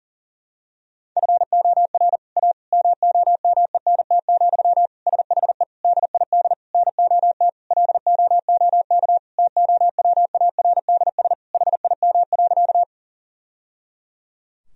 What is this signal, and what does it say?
Morse code, heard as a keyed tone, about 37 wpm, 700 Hz. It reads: FOR A MOMENT, SHE DID NOT LOOK TOWARDS HIM.
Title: Sentences from Top 1000 Words - Review Part 2 - 40wpm